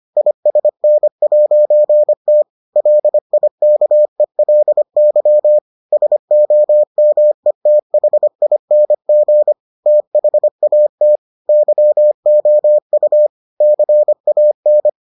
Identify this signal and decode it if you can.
Morse code, heard as a keyed tone, about 25 wpm, 600 Hz. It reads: ISN'T LIKELY SOMETHING THAT YOU CAN